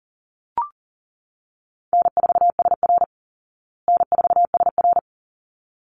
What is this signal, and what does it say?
Morse code, heard as a keyed tone, about 37 wpm, 700 Hz. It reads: N4SR N4SR
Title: Easy Call Signs Repeated 2x - 40wpm